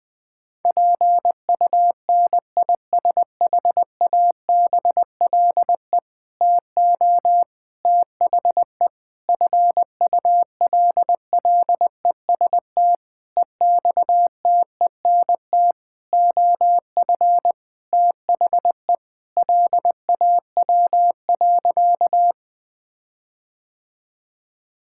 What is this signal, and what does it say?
Morse code, heard as a keyed tone, about 20 wpm, 700 Hz. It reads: PUNISHABLE TO THE FULLEST EXTENT OF THE LAW.
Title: Sentences with 7-Word Phrases - 20wpm